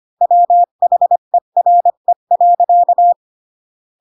Morse code, heard as a keyed tone, about 25 wpm, 700 Hz. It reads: WHERE.